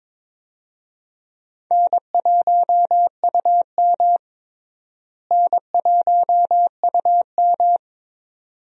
Morse code, heard as a keyed tone, about 22 wpm, 700 Hz. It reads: N1UM N1UM
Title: USA Call Signs - 4 Characters - Repeated 2x - 22wpm